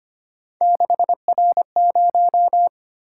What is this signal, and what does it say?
Morse code, heard as a keyed tone, about 30 wpm, 700 Hz. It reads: 6R0
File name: Speed-Racing - Three Letter-Number Combinations - 25wpm